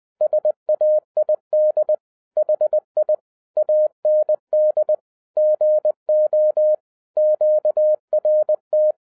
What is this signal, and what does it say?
Morse code, heard as a keyed tone, about 20 wpm, 610 Hz. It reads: SAID HI AND GO QRT
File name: 2012-01-mp3 / n16-THROUGH EIGHT MONTHS-20wpm